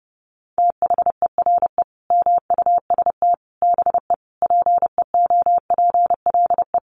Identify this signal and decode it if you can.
Morse code, heard as a keyed tone, about 30 wpm, 700 Hz. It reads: THERE MUST BE PEOPLE
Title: Sentences from Top 1400 Words 30wpm